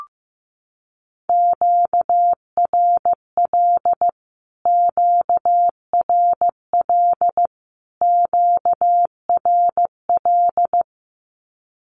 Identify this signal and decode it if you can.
Morse code, heard as a keyed tone, about 15 wpm, 700 Hz. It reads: QRL QRL QRL
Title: Q-Codes Repeated 3x 15wpm